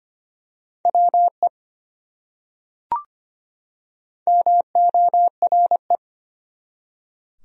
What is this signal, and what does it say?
Morse code, heard as a keyed tone, about 25 wpm, 700 Hz. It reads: WE MORE
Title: Top 100 Words 25wpm